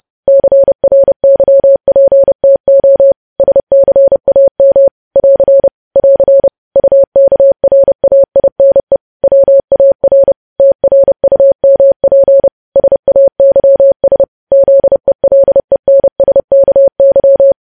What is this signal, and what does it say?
Morse code, heard as a keyed tone, about 30 wpm, 550 Hz. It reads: CRYPTO SCAM <AR> <AR> UKRAINE WAR TRUMP SAYS ZELENSKY
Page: news headlines in morse code at 30 wpm